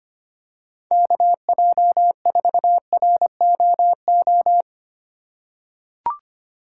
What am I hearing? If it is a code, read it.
Morse code, heard as a keyed tone, about 25 wpm, 700 Hz. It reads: KJ4ROO
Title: USA Call Signs - 6 Characters - 4x Character Spacing - 25wpm